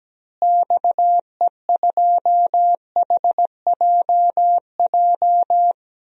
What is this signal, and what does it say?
Morse code, heard as a keyed tone, about 17 wpm, 700 Hz. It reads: XE2HJJ
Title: DXCC-100 Call Signs - 6 Characters - Repeated 3x - 17wpm